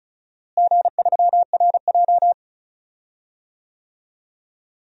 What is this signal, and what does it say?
Morse code, heard as a keyed tone, about 36 wpm, 700 Hz. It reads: G3RJ
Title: DXCC-100 Call Signs - 4 Characters - Repeated 2x - 35wpm